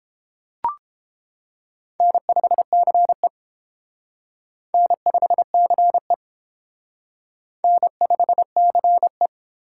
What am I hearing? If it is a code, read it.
Morse code, heard as a keyed tone, about 29 wpm, 700 Hz. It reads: N5CE N5CE N5CE